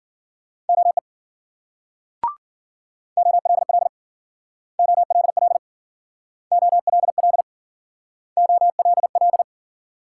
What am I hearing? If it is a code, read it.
Morse code, heard as a keyed tone, about 53 wpm, 700 Hz. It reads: OE OLL OLL OLL OLL